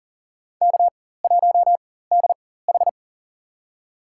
Morse code, heard as a keyed tone, about 40 wpm, spaced as for 19 wpm, 700 Hz. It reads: K1DH
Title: USA Call Signs - 4 Characters - 4x Character Spacing - 40wpm